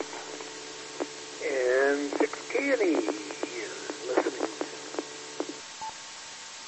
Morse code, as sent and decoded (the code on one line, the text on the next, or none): none